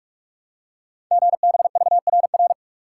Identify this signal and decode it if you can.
Morse code, heard as a keyed tone, about 44 wpm, 700 Hz. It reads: GBVRR
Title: Code Groups - 5 Letters - 45wpm